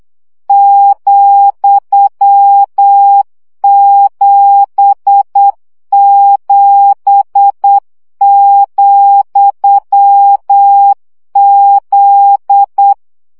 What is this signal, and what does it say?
Morse code, heard as a keyed tone, about 8 wpm, 800 Hz. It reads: ,77,Z
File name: CW gyakorlas z 7 vesszo 7 baud